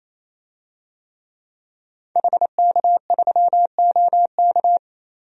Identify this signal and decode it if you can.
Morse code, heard as a keyed tone, about 28 wpm, 700 Hz. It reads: HK3OK
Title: DXCC-100 Call Signs - 5 Characters - 28wpm